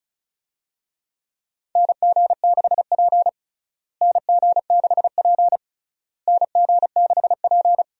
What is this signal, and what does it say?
Morse code, heard as a keyed tone, about 36 wpm, 700 Hz. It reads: NG6P NG6P NG6P
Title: USA Call Signs - 4 Characters - Repeated 3x - 35wpm